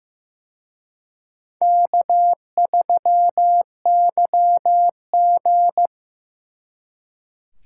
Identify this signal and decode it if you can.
Morse code, heard as a keyed tone, about 18 wpm, 700 Hz. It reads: K3YG
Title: Speed Racing - USA 1x2 Call Signs 15wpm